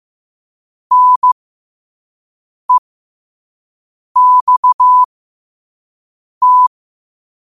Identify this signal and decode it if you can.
Morse code, heard as a keyed tone, about 15 wpm, spaced as for 5 wpm, 1000 Hz. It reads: NEXT
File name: qotdfiles / 5wpm